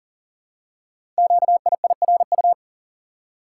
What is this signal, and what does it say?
Morse code, heard as a keyed tone, about 43 wpm, 700 Hz. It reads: QIIRU
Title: Code Groups - 5 Letters - 40wpm